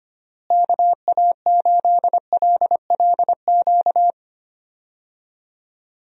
Morse code, heard as a keyed tone, about 25 wpm, 700 Hz. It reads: KA8LLQ
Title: USA Call Signs - 6 Characters - Repeated 3x - 25wpm